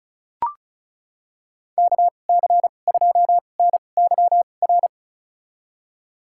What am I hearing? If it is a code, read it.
Morse code, heard as a keyed tone, about 36 wpm, spaced as for 25 wpm, 700 Hz. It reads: KC2NYR
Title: USA Call Signs - 2x Character Spacing - 35wpm